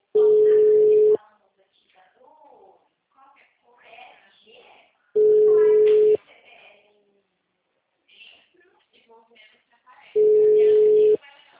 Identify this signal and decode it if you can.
Ringback tone (Europe, 425 Hz), with speech.